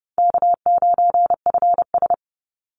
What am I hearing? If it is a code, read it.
Morse code, heard as a keyed tone, about 30 wpm, 700 Hz. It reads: K9FS
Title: Easy Call Signs - 30wpm